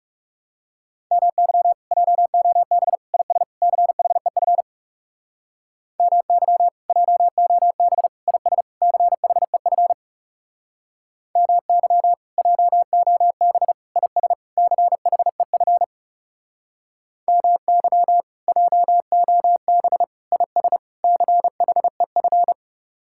Morse code, heard as a keyed tone, about 37 wpm, 700 Hz. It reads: MY JOB IS CHEF MY JOB IS CHEF MY JOB IS CHEF MY JOB IS CHEF
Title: Speed-Racing Common QSO Phrases - 30wpm